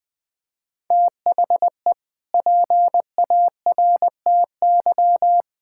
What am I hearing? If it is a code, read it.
Morse code, heard as a keyed tone, about 20 wpm, 700 Hz. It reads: THE PARTY
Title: Sentences with 2-Word Phrases - 20wpm